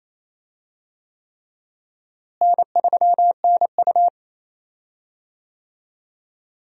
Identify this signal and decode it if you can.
Morse code, heard as a keyed tone, about 28 wpm, 700 Hz. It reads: N3NU
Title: USA Call Signs - 4 Characters - Repeated 3x - 28wpm